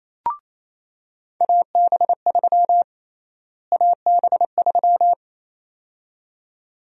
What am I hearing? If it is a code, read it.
Morse code, heard as a keyed tone, about 28 wpm, 700 Hz. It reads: AB3 AB3
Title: USA Call Sign Prefixes - Repeated 2x - 28wpm